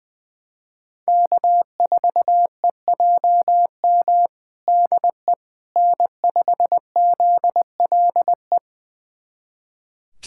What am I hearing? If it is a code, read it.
Morse code, heard as a keyed tone, about 20 wpm, 700 Hz. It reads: K4EJM DE N5ZLE
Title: USA Callsign DE Callsign - 5 Characters - 20wpm